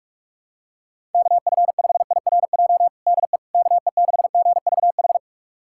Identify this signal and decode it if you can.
Morse code, heard as a keyed tone, about 42 wpm, 700 Hz. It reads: KF5IRJ DE KE6GVH
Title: USA Callsign DE Callsign - 6 Characters - 45wpm